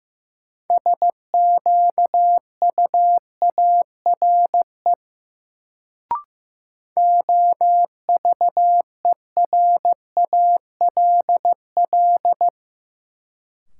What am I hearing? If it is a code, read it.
Morse code, heard as a keyed tone, about 15 wpm, 700 Hz. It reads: SQUARE OVERALL
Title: Sentences from Top 1000 Words - Review Part 7 - New Word Drill - 15wpm